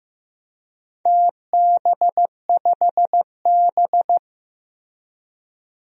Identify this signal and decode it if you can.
Morse code, heard as a keyed tone, about 15 wpm, 700 Hz. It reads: TB5B
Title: DXCC-100 Call Signs - 4 Characters - Repeated 2x - 15wpm